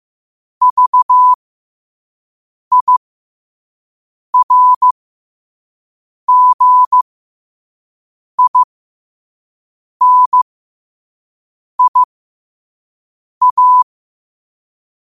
Morse code, heard as a keyed tone, about 15 wpm, spaced as for 5 wpm, 1000 Hz. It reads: VIRGINIA